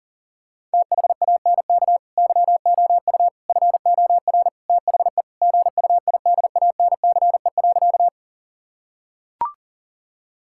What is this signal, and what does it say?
Morse code, heard as a keyed tone, about 37 wpm, 700 Hz. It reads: THANK YOU FOR THE GUIDANCE.
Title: Sentences from Top 1600 Words 40wpm